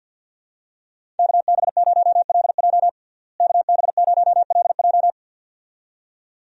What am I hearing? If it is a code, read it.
Morse code, heard as a keyed tone, about 50 wpm, 700 Hz. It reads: KB0LJ KB0LJ
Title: USA Call Signs - 5 Characters - Repeated 2x - 50wpm